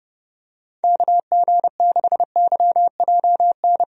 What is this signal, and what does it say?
Morse code, heard as a keyed tone, about 32 wpm, 700 Hz. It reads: KG6YJN
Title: USA Call Signs - 6 Characters - Repeated 2x - 30wpm